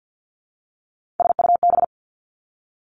Morse code, heard as a keyed tone, about 50 wpm, 700 Hz. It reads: SUB
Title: Three Challenging Letter-Number Combinations 50wpm